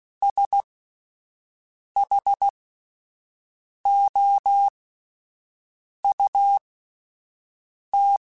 Morse code, heard as a keyed tone, about 16 wpm, spaced as for 5 wpm, 800 Hz. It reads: SHOUT
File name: S vs H / S vs H mp3s / S vs H @5